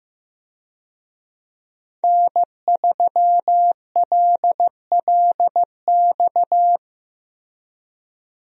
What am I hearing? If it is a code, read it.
Morse code, heard as a keyed tone, about 15 wpm, 700 Hz. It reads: N3LLX
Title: USA Call Signs - 5 Characters - Repeated 3x - 15wpm